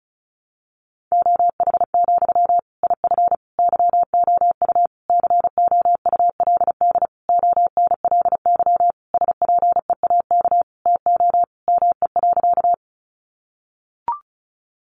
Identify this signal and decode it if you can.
Morse code, heard as a keyed tone, about 34 wpm, 700 Hz. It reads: OH, IF YOU COULD ONLY SPEAK TO ME.
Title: Sentences from Top 1100 Words 35wpm